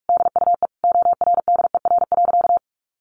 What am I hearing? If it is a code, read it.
Morse code, heard as a keyed tone, about 44 wpm, 700 Hz. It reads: DUE ORDER.
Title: Sentences from Top 900 Words 45wpm